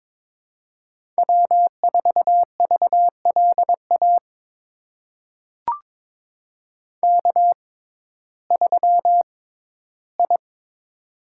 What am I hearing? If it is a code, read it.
Morse code, heard as a keyed tone, about 22 wpm, spaced as for 14 wpm, 700 Hz. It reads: W4VLA K3I
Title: USA Call Signs - 5 Characters - 6x Character Spacing - 22wpm